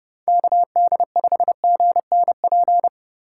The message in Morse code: KD5GNP

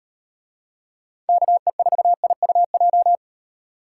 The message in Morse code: KE4IUJ

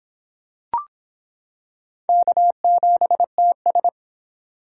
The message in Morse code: K7TS

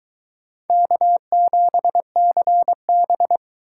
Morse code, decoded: K7CB

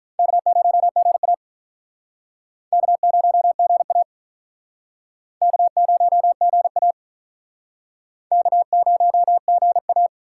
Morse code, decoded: K0GA K0GA K0GA K0GA

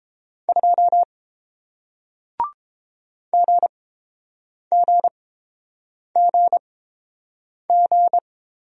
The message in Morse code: 2 GGGG

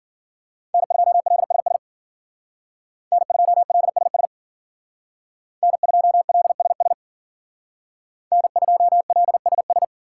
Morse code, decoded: N2LSS N2LSS N2LSS N2LSS

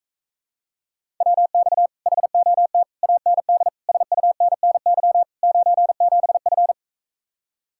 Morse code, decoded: WX HOT AND SUNNY 97F

